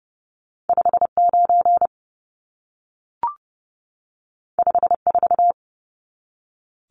Morse code, heard as a keyed tone, about 30 wpm, 700 Hz. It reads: 59 54
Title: Numbers - 2-Digit 30wpm